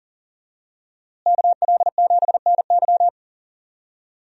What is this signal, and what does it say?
Morse code, heard as a keyed tone, about 39 wpm, 700 Hz. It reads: KL7NY
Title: USA Call Signs - 5 Characters - Repeated 3x - 40wpm